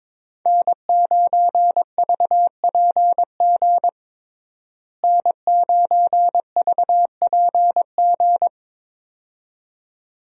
Morse code, heard as a keyed tone, about 22 wpm, 700 Hz. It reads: N9VPG N9VPG
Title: USA Call Signs - 5 Characters - Repeated 2x - 22wpm